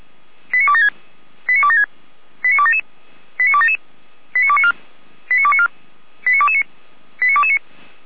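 Five-tone selective call (ZVEI), heard as quick runs of five tones, tones of about 70 ms. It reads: 89276 89276 89277 89277 89294 89294 89229 89229
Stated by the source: Received on an SDR, ZVEI five-tone call.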